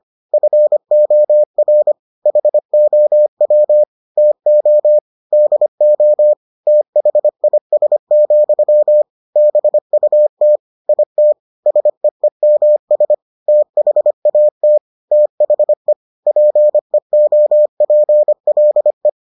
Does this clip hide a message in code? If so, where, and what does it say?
Yes: Morse code from 0.3 to 19.1 s, heard as a keyed tone, about 25 wpm, 600 Hz. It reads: FOR HOW TO DO THIS, BUT IT SEEMS THAT THE PEOPLE